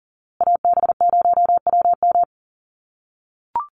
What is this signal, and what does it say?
Morse code, heard as a keyed tone, about 40 wpm, 700 Hz. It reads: AB0WM
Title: USA Call Signs Repeated 2x 40wpm